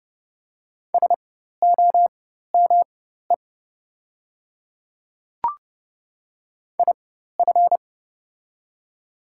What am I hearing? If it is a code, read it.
Morse code, heard as a keyed tone, about 30 wpm, spaced as for 14 wpm, 700 Hz. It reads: SOME IF
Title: 4x Character Spacing - Top 100 Words - 30wpm